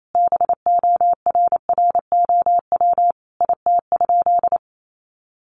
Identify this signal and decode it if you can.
Morse code, heard as a keyed tone, about 28 wpm, 700 Hz. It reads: BORROW IT?
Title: Sentences from Top 1500 Words 28wpm